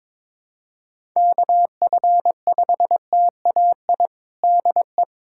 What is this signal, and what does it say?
Morse code, heard as a keyed tone, about 22 wpm, 700 Hz. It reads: KF5TAI DE
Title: USA Callsign DE Callsign 22wpm